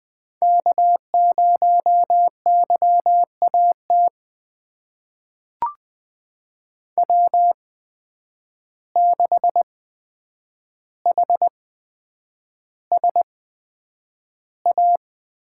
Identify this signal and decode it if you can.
Morse code, heard as a keyed tone, about 20 wpm, spaced as for 9 wpm, 700 Hz. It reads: K0YAT W6HSA
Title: USA Call Signs - 5 Characters - 8x Character Spacing - 20wpm